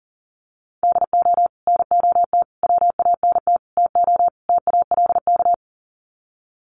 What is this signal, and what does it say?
Morse code, heard as a keyed tone, about 40 wpm, 700 Hz. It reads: DO NOT WANT TO TALK
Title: Sets of 5 Words 40wpm